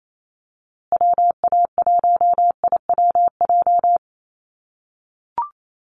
Morse code, heard as a keyed tone, about 28 wpm, 700 Hz. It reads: WA1IWJ
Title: USA Call Signs Repeated 2x 28wpm